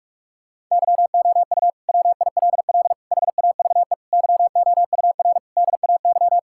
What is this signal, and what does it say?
Morse code, heard as a keyed tone, about 46 wpm, 700 Hz. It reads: YOU WILL HAVE YOUR DAY
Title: Sentences from Top 1600 Words 45wpm